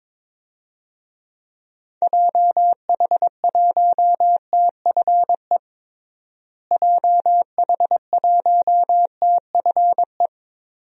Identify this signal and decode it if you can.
Morse code, heard as a keyed tone, about 21 wpm, 700 Hz. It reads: JH1TFE JH1TFE
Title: DXCC-100 Call Signs - 6 Characters - Repeated 2x - 22wpm